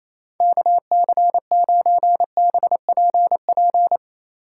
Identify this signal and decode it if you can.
Morse code, heard as a keyed tone, about 28 wpm, 700 Hz. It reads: KC9BPP
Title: USA Call Signs - 6 Characters - 3x Character Spacing - 28wpm